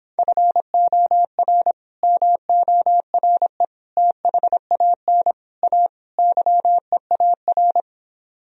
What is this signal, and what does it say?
Morse code, heard as a keyed tone, about 26 wpm, 700 Hz. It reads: FOR MORE THAN A YEAR